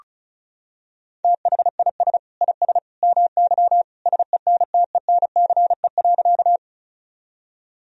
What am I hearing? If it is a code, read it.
Morse code, heard as a keyed tone, about 36 wpm, 690 Hz. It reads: THIS IS MY SENTENCE.
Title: Sentences from Top 1000 Words - Review Part 1 - 35wpm